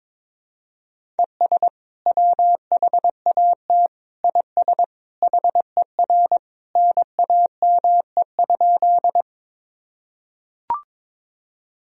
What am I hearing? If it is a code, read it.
Morse code, heard as a keyed tone, about 22 wpm, 700 Hz. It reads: ES WHAT IS HER NAME?
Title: Sentences from Top 200 Words and Abbreviations 22wpm